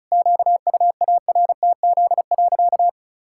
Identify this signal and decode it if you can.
Morse code, heard as a keyed tone, about 34 wpm, 700 Hz. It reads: QUARTZ.
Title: Pangram Challenge v2 - 35wpm